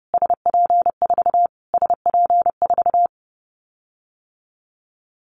Morse code, heard as a keyed tone, about 30 wpm, 700 Hz. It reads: SP4 SP4
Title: Callsign Prefixes for Top 25 DXCC - Repeated 3x - 30wpm